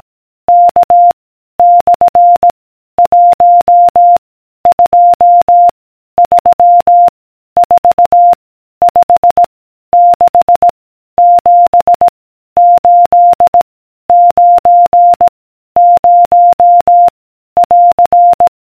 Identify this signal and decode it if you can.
Morse code, heard as a keyed tone, about 18 wpm, 700 Hz. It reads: K / 1 2 3 4 5 6 7 8 9 0 <AR>